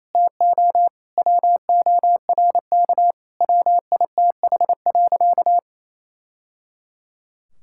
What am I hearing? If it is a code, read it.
Morse code, heard as a keyed tone, about 28 wpm, 700 Hz. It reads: TO WORK WITH.
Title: Sentences from Top 1000 Words - Review Part 5 - 28wpm